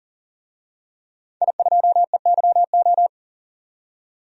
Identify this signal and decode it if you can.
Morse code, heard as a keyed tone, about 40 wpm, 700 Hz. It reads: I2EYO
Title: DXCC-100 Call Signs - 5 Characters - 40wpm